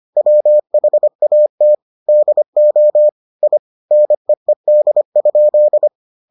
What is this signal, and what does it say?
Morse code, heard as a keyed tone, about 25 wpm, 600 Hz. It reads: WHAT DO I NEED?